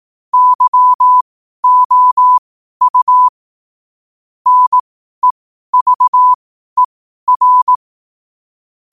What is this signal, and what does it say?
Morse code, heard as a keyed tone, about 18 wpm, spaced as for 12 wpm, 1000 Hz. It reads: YOU NEVER